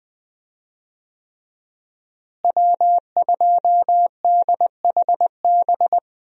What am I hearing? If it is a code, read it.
Morse code, heard as a keyed tone, about 20 wpm, 700 Hz. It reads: W2DHB
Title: USA Call Signs - 5 Characters - Repeated 3x - 20wpm